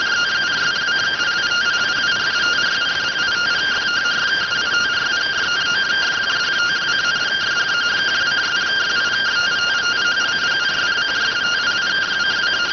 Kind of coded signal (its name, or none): two-tone teleprinter signal